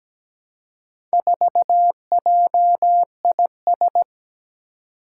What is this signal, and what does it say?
Morse code, heard as a keyed tone, about 17 wpm, 700 Hz. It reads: 4JIS
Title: Four Challenging Letter-Number Combinations 17wpm